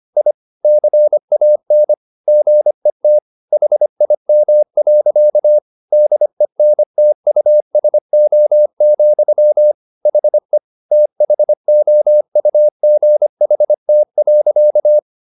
Morse code, heard as a keyed tone, about 25 wpm, 600 Hz. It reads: I CAN GET HIM. DENTUSO, HE THOUGHT.